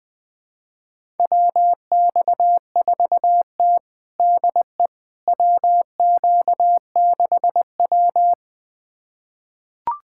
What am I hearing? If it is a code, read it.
Morse code, heard as a keyed tone, about 20 wpm, 700 Hz. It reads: WX4T DE WQ6W
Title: USA Callsign DE Callsign - 4 Characters - 20wpm